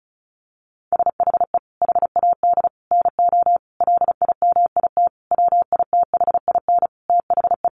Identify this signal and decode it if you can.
Morse code, heard as a keyed tone, about 34 wpm, 700 Hz. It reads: SHE HAD NO LIMIT WITHIN THE
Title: Sentences from Top 1500 Words 35wpm